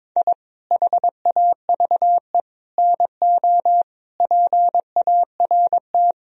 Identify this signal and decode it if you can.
Morse code, heard as a keyed tone, about 21 wpm, 700 Hz. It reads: I HAVE NO PART
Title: Sentences from Top 1000 Words - Review Part 1 - 22wpm